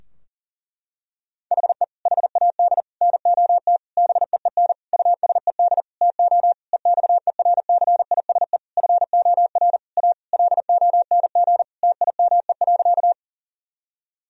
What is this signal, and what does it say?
Morse code, heard as a keyed tone, about 40 wpm, 700 Hz. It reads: HE HAD NOT BEEN USED TO EXERCISE FOR A LONG TIME.
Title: Sentences from Top 1000 Words - Review Part 4 - 40wpm